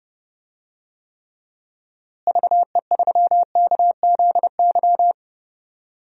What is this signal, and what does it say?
Morse code, heard as a keyed tone, about 29 wpm, 700 Hz. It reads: VE3KZY
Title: DXCC-100 Call Signs - 6 Characters - Repeated 2x - 30wpm